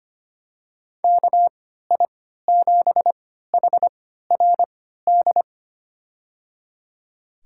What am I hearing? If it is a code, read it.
Morse code, heard as a keyed tone, about 25 wpm, spaced as for 14 wpm, 700 Hz. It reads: KI7HRD
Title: USA Call Signs - 6 Characters - 3x Character Spacing - 25wpm